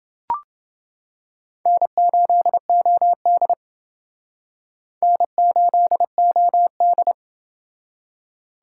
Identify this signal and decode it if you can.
Morse code, heard as a keyed tone, about 29 wpm, 700 Hz. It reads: N8OD N8OD